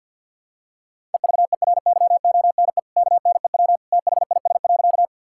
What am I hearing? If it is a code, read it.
Morse code, heard as a keyed tone, about 49 wpm, 700 Hz. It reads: EVERYONE KNEW THIS.
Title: Sentences from Top 1000 Words - Review Part 5 - 50wpm